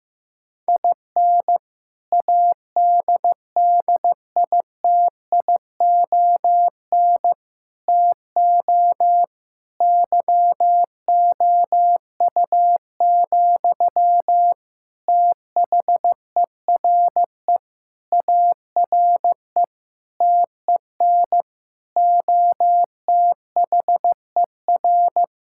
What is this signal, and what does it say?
Morse code, heard as a keyed tone, about 15 wpm, 700 Hz. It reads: IN ADDITION TO YOU, THERE ARE TEN OTHER
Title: Sentences from Top 1000 Words - Review Part 6 - 15wpm